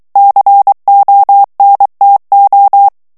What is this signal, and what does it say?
Morse code, heard as a keyed tone, about 23 wpm, 800 Hz. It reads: CONTO
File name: ITALIEN texte clair 1 vitesse 22 mots minute